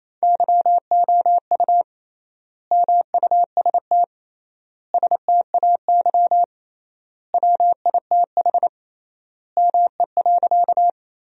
Morse code, heard as a keyed tone, about 28 wpm, 700 Hz. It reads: YOU MUST STAY WITH ME.
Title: 3x Word-Spacing - Sentences from Top 500 Words - 28wpm